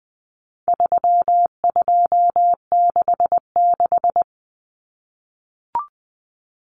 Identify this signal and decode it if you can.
Morse code, heard as a keyed tone, about 20 wpm, 700 Hz. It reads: 3266
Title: Numbers - 4-Digit 20wpm